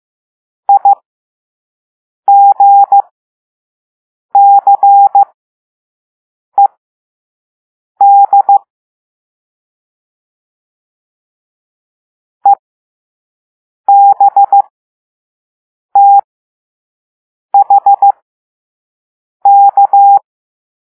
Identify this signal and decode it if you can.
Morse code, heard as a keyed tone, about 15 wpm, spaced as for 5 wpm, 800 Hz. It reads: IGCED EBTHK